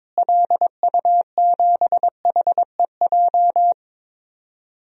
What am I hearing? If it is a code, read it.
Morse code, heard as a keyed tone, about 22 wpm, 700 Hz. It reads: LU7HEJ
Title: DXCC-100 Call Signs - 6 Characters - Repeated 3x - 22wpm